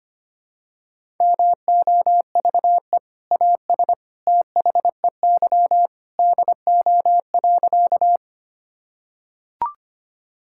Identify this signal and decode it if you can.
Morse code, heard as a keyed tone, about 25 wpm, 700 Hz. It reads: MOVE AS THEY DO.
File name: practice / 25wpm / Sentences with 2-Word Phrases - 25wpm